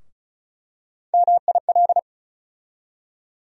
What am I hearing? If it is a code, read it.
Morse code, heard as a keyed tone, about 34 wpm, 700 Hz. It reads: MIL
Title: USA Call Sign Postfixes - 3 Letters 35wpm